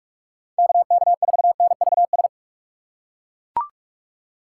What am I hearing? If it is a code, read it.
Morse code, heard as a keyed tone, about 40 wpm, 700 Hz. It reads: KK4NVS